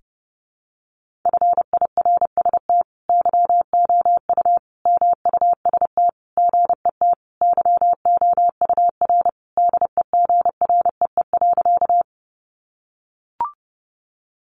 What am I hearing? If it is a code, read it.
Morse code, heard as a keyed tone, about 30 wpm, 700 Hz. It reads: FIRST YOU MUST GET YOUR DEGREE.
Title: Sentences from Top 1500 Words 30wpm